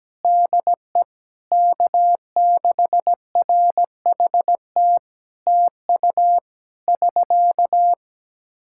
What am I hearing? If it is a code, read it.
Morse code, heard as a keyed tone, about 17 wpm, 700 Hz. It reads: DE K6RHT TU <SK>
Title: Practice QSO - Exchange 4 - Ending - 17wpm